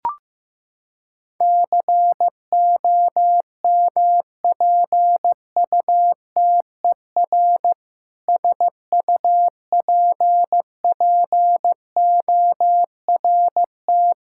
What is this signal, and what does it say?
Morse code, heard as a keyed tone, about 15 wpm, 700 Hz. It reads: COMPUTER SUPPORT
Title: Practice QSO - Exchange 3 - More Info - 15wpm